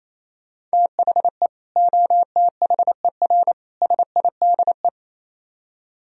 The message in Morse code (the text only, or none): THE OTHER SIDE